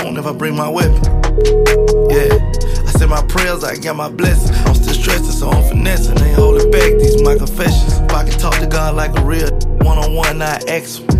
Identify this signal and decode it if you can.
Ringback tone (Europe, 425 Hz), with speech.